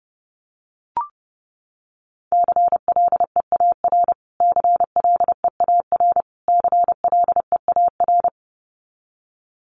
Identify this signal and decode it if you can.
Morse code, heard as a keyed tone, about 30 wpm, 700 Hz. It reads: CLEAR CLEAR CLEAR
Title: Top 500 Words Repeated 30wpm